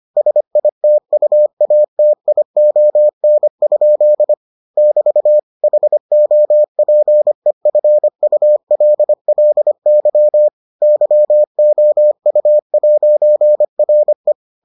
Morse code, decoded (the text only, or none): SITUATION? <BT> HOPEFULLY YOU'RE